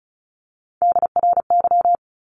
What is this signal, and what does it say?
Morse code, heard as a keyed tone, about 34 wpm, 700 Hz. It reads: DRY
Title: Top 500 Words 35wpm